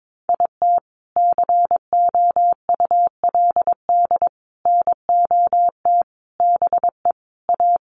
Morse code, heard as a keyed tone, about 22 wpm, 700 Hz. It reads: IT COULD NOT BE A
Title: Sentences from Top 1500 Words 22wpm